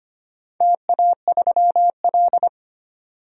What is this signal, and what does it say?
Morse code, heard as a keyed tone, about 25 wpm, 700 Hz. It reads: TA3L
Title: DXCC-100 Call Signs - 4 Characters - 25wpm